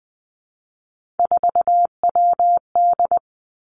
Morse code, heard as a keyed tone, about 20 wpm, 700 Hz. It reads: 4WD
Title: Three Challenging Letter-Number Combinations 20wpm